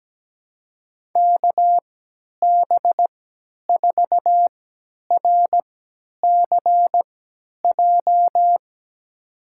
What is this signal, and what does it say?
Morse code, heard as a keyed tone, about 17 wpm, spaced as for 10 wpm, 700 Hz. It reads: KB4RCJ
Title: USA Call Signs - 6 Characters - 3x Character Spacing - 17wpm